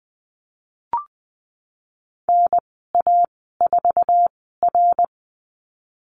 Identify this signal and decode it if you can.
Morse code, heard as a keyed tone, about 20 wpm, spaced as for 15 wpm, 700 Hz. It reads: NA4R
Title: 088 Lesson 15g - Number 5 - Call Signs 20wpm